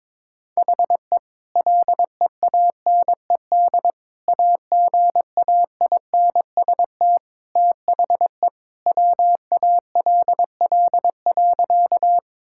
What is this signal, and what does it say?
Morse code, heard as a keyed tone, about 22 wpm, 700 Hz. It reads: HE LEANED AGAINST THE WALL.
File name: practice / 22wpm / Sentences with 2-Word Phrases - 22wpm